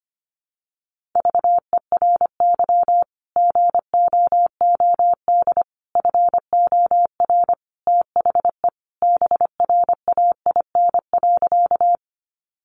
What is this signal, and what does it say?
Morse code, heard as a keyed tone, about 25 wpm, 700 Hz. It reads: VERY GOOD FOR THE BRAIN.
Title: Sentences from Top 1100 Words 25wpm